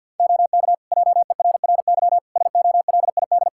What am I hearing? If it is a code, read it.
Morse code, heard as a keyed tone, about 47 wpm, 700 Hz. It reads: OK JERRY SOLID
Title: Practice QSO - Exchange 3 - More Info - 50wpm